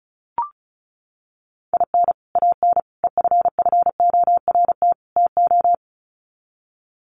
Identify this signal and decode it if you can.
Morse code, heard as a keyed tone, about 34 wpm, 700 Hz. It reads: IN AN EFFORT TO
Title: Sets of 4 Words 35wpm